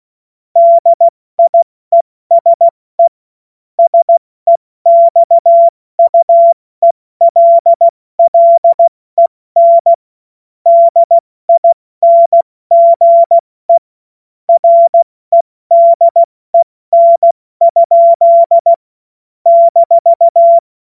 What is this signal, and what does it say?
Morse code, heard as a keyed tone, about 16 wpm, 680 Hz. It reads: DIESE SEXUELLEN DINGE REDEN? -
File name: Morsekurs / Witze / Witz 16 April 2024 70 F80